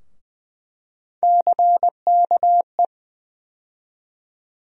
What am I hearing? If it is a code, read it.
Morse code, heard as a keyed tone, about 20 wpm, 700 Hz. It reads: CKE
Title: USA Call Sign Postfixes - 3 Letters 20wpm